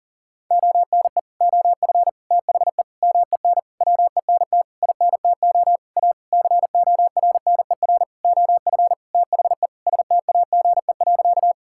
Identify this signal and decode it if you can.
Morse code, heard as a keyed tone, about 40 wpm, 700 Hz. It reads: ONE OF THE MEN WENT INTO A CORNER OF THE STAGE.
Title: Sentences from Top 1600 Words 40wpm